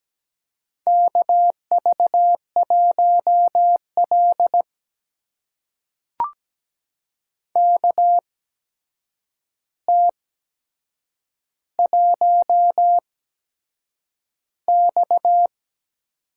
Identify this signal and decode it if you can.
Morse code, heard as a keyed tone, about 17 wpm, spaced as for 7 wpm, 700 Hz. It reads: KV1L KT1X
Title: USA Call Signs - 4 Characters - 8x Character Spacing - 17wpm